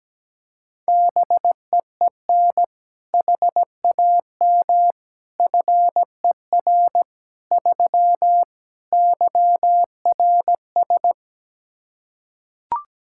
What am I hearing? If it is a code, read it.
Morse code, heard as a keyed tone, about 17 wpm, 700 Hz. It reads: BEEN HAM FER 3 YRS
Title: Practice QSO - Exchange 3 - More Info - 17wpm